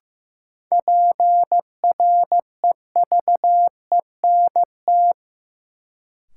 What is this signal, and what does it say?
Morse code, heard as a keyed tone, about 15 wpm, 700 Hz. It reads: PREVENT
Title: Sentences from Top 1100 Words - New Word Drill Alt - 15wpm